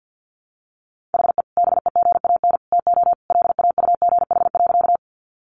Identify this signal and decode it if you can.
Morse code, heard as a keyed tone, about 50 wpm, 700 Hz. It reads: HE BEGAN TO LAUGH.
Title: Sentences from Top 500 Words 50wpm